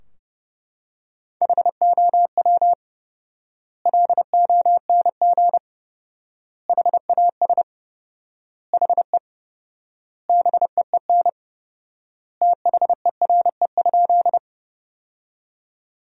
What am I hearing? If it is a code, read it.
Morse code, heard as a keyed tone, about 32 wpm, 700 Hz. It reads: HOW LONG HAS HE BEEN THERE?
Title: 4x Word-Spacing - Sentences from Top 100 Words - 30wpm